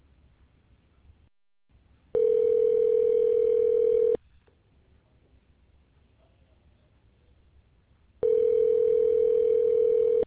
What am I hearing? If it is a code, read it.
Ringback tone (North America), 440 + 480 Hz.